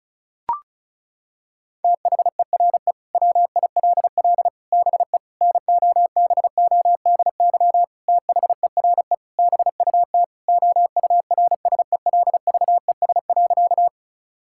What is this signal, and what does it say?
Morse code, heard as a keyed tone, about 36 wpm, 700 Hz. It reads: THERE WILL BE NOBODY THERE BUT OURSELVES.
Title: Sentences from Top 1600 Words 35wpm